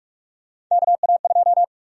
Morse code, heard as a keyed tone, about 46 wpm, 700 Hz. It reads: KA2
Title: USA Call Sign Prefixes - 45wpm